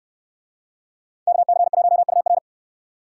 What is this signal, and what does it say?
Morse code, heard as a keyed tone, about 69 wpm, 700 Hz. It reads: KC1UR